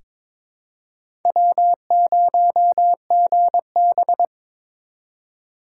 Morse code, heard as a keyed tone, about 22 wpm, 700 Hz. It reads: W0GB
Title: USA Call Signs - 4 Characters - 3x Character Spacing - 22wpm